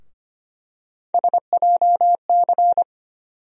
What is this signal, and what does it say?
Morse code, heard as a keyed tone, about 25 wpm, 700 Hz. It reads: SJC
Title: USA Call Sign Postfixes - 3 Letters 25wpm